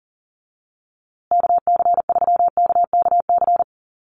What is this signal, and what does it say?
Morse code, heard as a keyed tone, about 40 wpm, 700 Hz. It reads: KC3KKC
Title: USA Call Signs Repeated 3x 40wpm